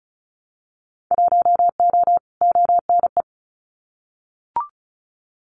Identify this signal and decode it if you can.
Morse code, heard as a keyed tone, about 34 wpm, 700 Hz. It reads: 1O ONE
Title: Winter Field Day Exchanges 35wpm